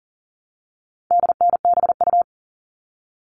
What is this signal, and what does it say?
Morse code, heard as a keyed tone, about 40 wpm, 700 Hz. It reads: DNBU
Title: All Cut Numbers - 4-Digit 40wpm